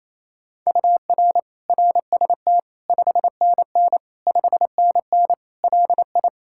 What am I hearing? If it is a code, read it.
Morse code, heard as a keyed tone, about 28 wpm, 700 Hz. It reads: UR RST 5NN 5NN LI
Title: Practice QSO - Exchange 1 - Beginning - 28wpm